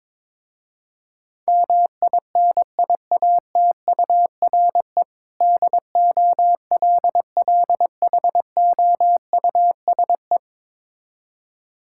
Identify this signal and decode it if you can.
Morse code, heard as a keyed tone, about 22 wpm, 700 Hz. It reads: MINIATURE DOLLHOUSE
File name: Difficult Words with Hints - 22wpm